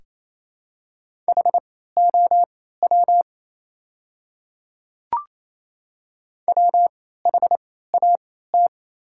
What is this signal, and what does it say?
Morse code, heard as a keyed tone, about 28 wpm, spaced as for 16 wpm, 700 Hz. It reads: HOW WHAT
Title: 3x Character Spacing - Top 300 Words - 28wpm